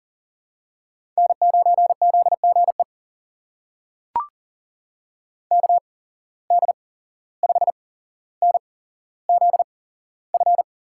Morse code, heard as a keyed tone, about 39 wpm, spaced as for 16 wpm, 700 Hz. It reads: N9ZGE KD5NZF
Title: USA Call Signs - 8x Character Spacing - 40wpm